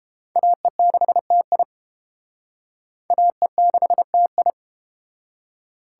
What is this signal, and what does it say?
Morse code, heard as a keyed tone, about 31 wpm, 700 Hz. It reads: AE6TI AE6TI